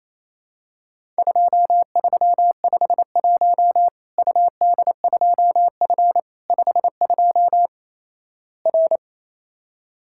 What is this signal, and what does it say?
Morse code, heard as a keyed tone, about 28 wpm, 700 Hz. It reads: 2351 UD2F 52 R
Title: Worked All Europe DX Contest - QTC Exchange - 28wpm